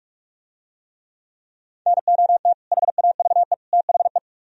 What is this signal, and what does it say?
Morse code, heard as a keyed tone, about 44 wpm, 700 Hz. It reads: NOT HAVE THE